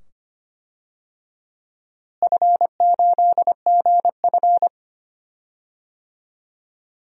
Morse code, heard as a keyed tone, about 25 wpm, 700 Hz. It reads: F8GF
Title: DXCC-100 Call Signs - 4 Characters - Repeated 2x - 25wpm